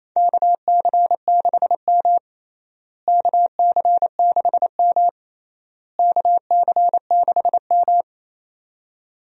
Morse code, heard as a keyed tone, about 28 wpm, 700 Hz. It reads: KC6M KC6M KC6M
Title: USA Call Signs - 4 Characters - Repeated 3x - 28wpm